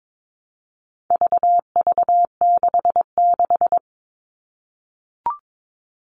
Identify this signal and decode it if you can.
Morse code, heard as a keyed tone, about 22 wpm, 700 Hz. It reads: VV66
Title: All Cut Numbers - 4-Digit 22wpm